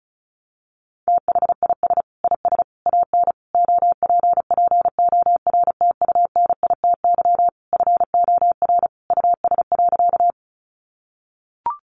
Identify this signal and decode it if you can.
Morse code, heard as a keyed tone, about 34 wpm, 700 Hz. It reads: THIS IS AN OPPORTUNITY FOR US.
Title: Sentences from Top 1000 Words 35wpm